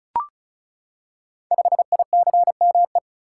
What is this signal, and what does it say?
Morse code, heard as a keyed tone, about 36 wpm, 690 Hz. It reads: 5ICME
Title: Code Groups - 5 Letter:Numbers - 35wpm